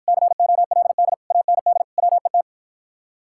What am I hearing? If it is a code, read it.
Morse code, heard as a keyed tone, about 54 wpm, 700 Hz. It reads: COLD AND WET